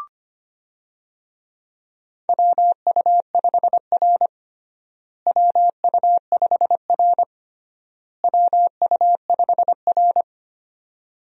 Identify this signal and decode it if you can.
Morse code, heard as a keyed tone, about 25 wpm, 700 Hz. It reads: WU5R WU5R WU5R
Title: USA Call Signs - 4 Characters - Repeated 3x - 25wpm